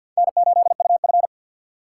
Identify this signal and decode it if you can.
Morse code, heard as a keyed tone, about 53 wpm, 700 Hz. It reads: N8UF
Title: USA Call Signs - 4 Characters - 6x Character Spacing - 50wpm